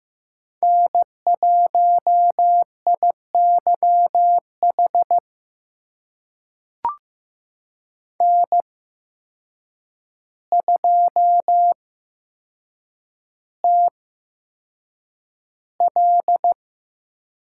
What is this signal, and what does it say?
Morse code, heard as a keyed tone, about 15 wpm, spaced as for 7 wpm, 700 Hz. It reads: N1IYH N2TL
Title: USA Call Signs - 5 Characters - 8x Character Spacing - 15wpm